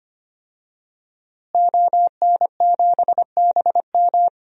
Morse code, heard as a keyed tone, about 25 wpm, 700 Hz. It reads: ON7BM
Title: DXCC-100 Call Signs - 5 Characters - Repeated 3x - 25wpm